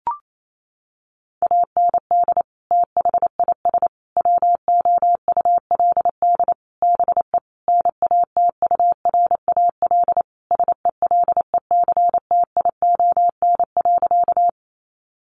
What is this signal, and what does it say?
Morse code, heard as a keyed tone, about 28 wpm, 700 Hz. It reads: AND THIS WOULD BE NATURAL SELECTION.
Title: Sentences from Top 1400 Words 28wpm